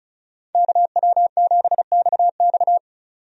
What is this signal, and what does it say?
Morse code, heard as a keyed tone, about 36 wpm, 700 Hz. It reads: KW7XX
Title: USA Call Signs - 5 Characters - 35wpm